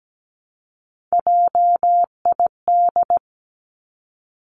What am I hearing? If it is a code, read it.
Morse code, heard as a keyed tone, about 17 wpm, 700 Hz. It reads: JID
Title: Three Challenging Letter-Number Combinations 17wpm